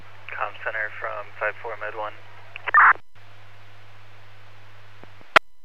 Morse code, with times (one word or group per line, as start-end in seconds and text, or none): none